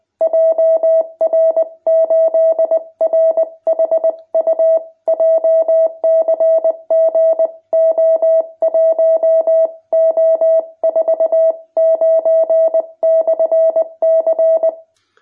Morse code, heard as a keyed tone, about 20 wpm, 660 Hz. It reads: JR8RHU JCG O1O49/C
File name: jh8cbh / 2021cw